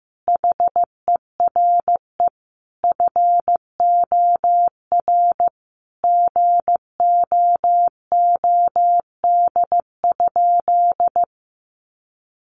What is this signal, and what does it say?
Morse code, heard as a keyed tone, about 15 wpm, 700 Hz. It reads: HERE FOR GOOD?
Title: Sentences from Top 200 Words 15wpm